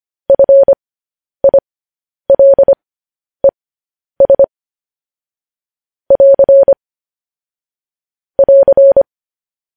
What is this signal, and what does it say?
Morse code, heard as a keyed tone, about 25 wpm, spaced as for 10 wpm, 550 Hz. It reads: FILES <AR> <AR>